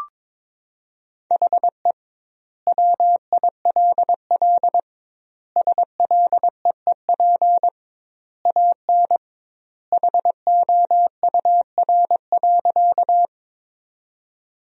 Morse code, heard as a keyed tone, about 22 wpm, 700 Hz. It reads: HE WILL SLEEP AN HOUR.
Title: 2x Word-Spacing - Sentences from Top 500 Words - 22wpm